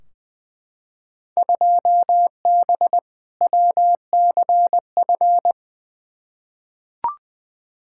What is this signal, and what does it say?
Morse code, heard as a keyed tone, about 20 wpm, 700 Hz. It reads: 2B WCF
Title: Field Day Exchanges Encore - 20wpm